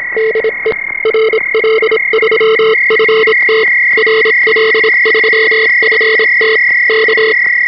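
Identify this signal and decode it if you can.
Morse code, heard as a keyed tone, about 27 wpm, 440 Hz, with noise. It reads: DE RL3FT RL3FT K
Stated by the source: 3.5 MHz CW received on an experimental direct digital HF receiver with RF band pass filters for the 3.5 and 7 MHz ham bands.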